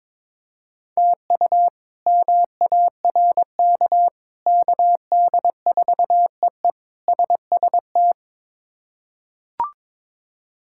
Morse code, heard as a keyed tone, about 22 wpm, 700 Hz. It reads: TU MARK KD4EE SST
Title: K1USN SST QSO Practice - 22wpm